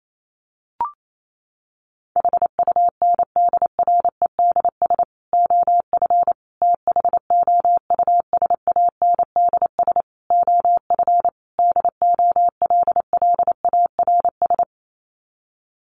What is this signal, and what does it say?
Morse code, heard as a keyed tone, about 28 wpm, 700 Hz. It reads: HUNDREDS OF THOUSANDS OF DOLLARS
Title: Sets of 5 Words Encore 28wpm